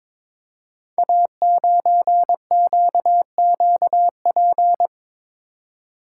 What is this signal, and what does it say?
Morse code, heard as a keyed tone, about 22 wpm, 700 Hz. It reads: A9QQP
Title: Code Groups - 5 Letter:Numbers - 22wpm